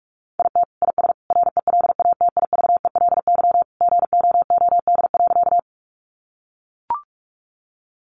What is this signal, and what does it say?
Morse code, heard as a keyed tone, about 44 wpm, 700 Hz. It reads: IT IS RELATIVELY GOOD.
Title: Sentences from Top 1100 Words 45wpm